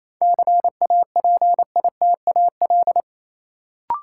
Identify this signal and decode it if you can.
Morse code, heard as a keyed tone, about 28 wpm, 700 Hz. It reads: CAPITAL
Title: Sentences from Top 1000 Words - New Word Drill - 28wpm